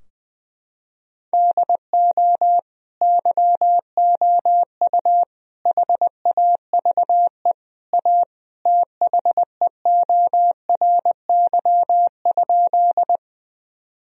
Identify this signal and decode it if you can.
Morse code, heard as a keyed tone, about 19 wpm, 700 Hz. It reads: DO YOU HAVE A THEORY?
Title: Sentences from Top 1000 Words - Review Part 3 - 20wpm